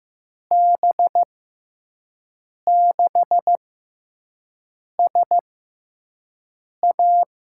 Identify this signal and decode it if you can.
Morse code, heard as a keyed tone, about 15 wpm, spaced as for 5 wpm, 700 Hz. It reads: B6SA
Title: USA Call Signs - 6x Character Spacing - 15wpm